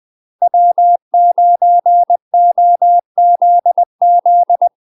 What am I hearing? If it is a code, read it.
Morse code, heard as a keyed tone, about 20 wpm, 700 Hz. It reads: W9OZZ
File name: UK MASTER QSOs TRAINING / 20WPM / UK MASTER QSOs 0066